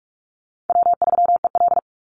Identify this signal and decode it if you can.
Morse code, heard as a keyed tone, about 46 wpm, 700 Hz. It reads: W3EL
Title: Easy Call Signs - 45wpm